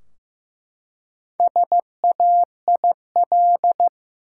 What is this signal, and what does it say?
Morse code, heard as a keyed tone, about 16 wpm, 700 Hz. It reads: SAIL
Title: Four Letter Words - 15wpm